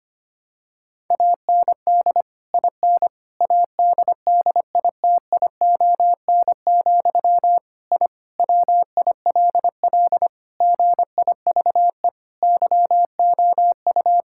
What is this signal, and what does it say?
Morse code, heard as a keyed tone, about 25 wpm, 700 Hz. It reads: AND IN ADDITION, I WILL GIVE YOU
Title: Sentences from Top 1000 Words - Review Part 6 - 25wpm